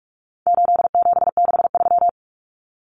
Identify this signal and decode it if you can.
Morse code, heard as a keyed tone, about 44 wpm, 700 Hz. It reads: 8763
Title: Common Cut Numbers - 4-Digit 45wpm